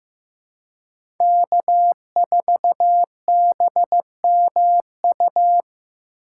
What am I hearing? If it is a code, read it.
Morse code, heard as a keyed tone, about 18 wpm, 700 Hz. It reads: K4BMU
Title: Speed Racing - USA 1x3 Call Signs 15wpm